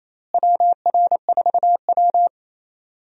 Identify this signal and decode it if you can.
Morse code, heard as a keyed tone, about 28 wpm, 700 Hz. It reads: WR4W
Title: USA Call Signs - 4 Characters - Repeated 3x - 28wpm